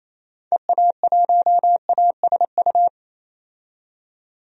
Morse code, heard as a keyed tone, about 28 wpm, 700 Hz. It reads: EA1ASU
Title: DXCC-100 Call Signs - 6 Characters - 28wpm